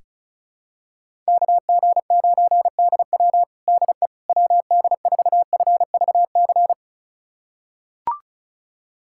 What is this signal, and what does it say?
Morse code, heard as a keyed tone, about 36 wpm, 700 Hz. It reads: KG9DW DE WD4FVC
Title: USA Callsign DE Callsign 35wpm